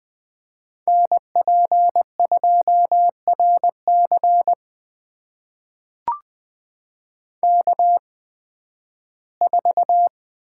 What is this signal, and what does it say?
Morse code, heard as a keyed tone, about 20 wpm, spaced as for 13 wpm, 700 Hz. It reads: NP2RC K4
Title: USA Call Signs - 5 Characters - 8x Character Spacing - 20wpm